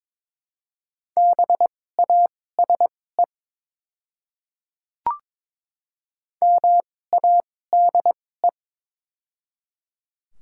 Morse code, heard as a keyed tone, about 22 wpm, spaced as for 16 wpm, 700 Hz. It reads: BASE MADE
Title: 2x Character Spacing - Top 300 Words - 22wpm